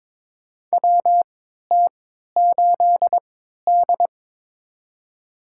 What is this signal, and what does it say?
Morse code, heard as a keyed tone, about 22 wpm, spaced as for 13 wpm, 700 Hz. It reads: WT8D
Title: USA Call Signs - 3x Character Spacing - 22wpm